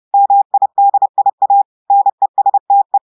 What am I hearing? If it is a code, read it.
Morse code, heard as a keyed tone, about 30 wpm, 800 Hz. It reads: MIDIA NESTE
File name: audios / portugais / portugais texte en clair 1 vitesse 30 mots minutes